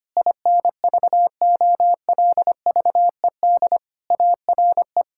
Morse code, heard as a keyed tone, about 25 wpm, 700 Hz. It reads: INVOLVED ARE
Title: Sentences from Top 1000 Words - Review Part 5 - 25wpm